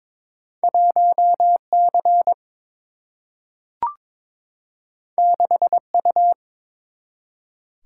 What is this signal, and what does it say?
Morse code, heard as a keyed tone, about 22 wpm, 700 Hz. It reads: 1 C 6 U
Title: Single-Letter Single-Number 22wpm